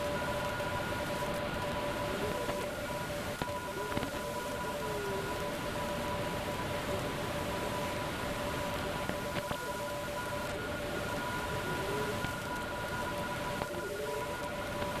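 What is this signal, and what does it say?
Two-tone teleprinter signal, about 850 Hz shift.
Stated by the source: Teleprinter signal N57K from a numbers station.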